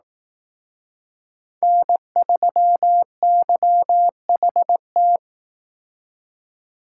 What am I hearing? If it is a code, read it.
Morse code, heard as a keyed tone, about 18 wpm, 700 Hz. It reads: N3YHT
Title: Speed Racing - USA 1x3 Call Signs 15wpm